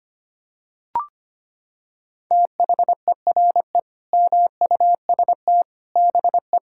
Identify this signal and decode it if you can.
Morse code, heard as a keyed tone, about 25 wpm, 700 Hz. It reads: THERE MUST BE
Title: Sentences from Top 1000 Words - Review Part 6 - 25wpm